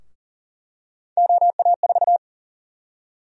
Morse code, heard as a keyed tone, about 39 wpm, 700 Hz. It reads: OA4
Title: USA Call Sign Prefixes - Repeated 2x - 40wpm